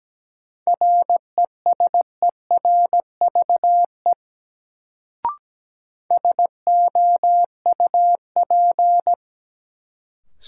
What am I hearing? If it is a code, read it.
Morse code, heard as a keyed tone, about 17 wpm, 700 Hz. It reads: RESERVE SOUP
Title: Sentences from Top 1000 Words - Review Part 6 - New Word Drill - 17wpm